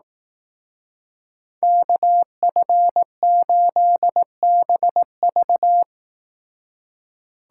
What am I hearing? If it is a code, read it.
Morse code, heard as a keyed tone, about 18 wpm, 700 Hz. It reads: KF8BV
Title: Speed Racing - USA 2x2 Call Signs 15wpm